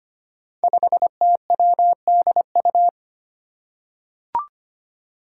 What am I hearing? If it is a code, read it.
Morse code, heard as a keyed tone, about 25 wpm, 700 Hz. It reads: 5TWDU
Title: Five Challenging Letter-Number Combinations 25wpm